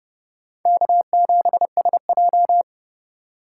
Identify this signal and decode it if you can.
Morse code, heard as a keyed tone, about 30 wpm, 700 Hz. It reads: K7SJ
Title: USA Call Signs - 4 Characters - 4x Character Spacing - 30wpm